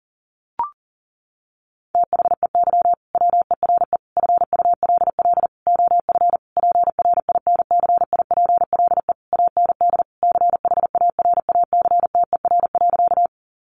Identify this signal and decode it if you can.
Morse code, heard as a keyed tone, about 40 wpm, 700 Hz. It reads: THEY WERE FULL OF PRINCIPLE AND CHARACTER.
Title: Sentences from Top 1400 Words 40wpm